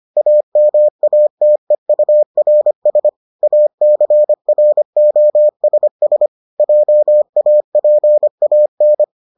Morse code, heard as a keyed tone, about 25 wpm, 600 Hz. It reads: AMATEURS ACROSS JAPAN